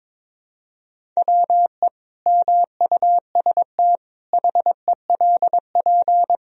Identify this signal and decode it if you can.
Morse code, heard as a keyed tone, about 22 wpm, 700 Hz. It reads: WE MUST HELP
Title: Sentences from Top 1000 Words - Review Part 1 - 22wpm